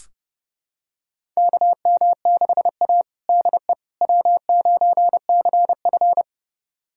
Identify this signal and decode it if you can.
Morse code, heard as a keyed tone, about 30 wpm, 700 Hz. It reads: KM6A DE W9CF
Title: USA Callsign DE Callsign - 4 Characters - 30wpm